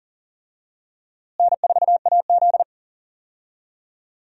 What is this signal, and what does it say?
Morse code, heard as a keyed tone, about 40 wpm, 700 Hz. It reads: N4AZ
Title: USA Call Signs - 4 Characters - Repeated 2x - 40wpm